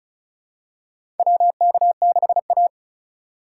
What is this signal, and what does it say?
Morse code, heard as a keyed tone, about 36 wpm, 700 Hz. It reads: WK6A
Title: USA Call Signs - 4x Character Spacing - 35wpm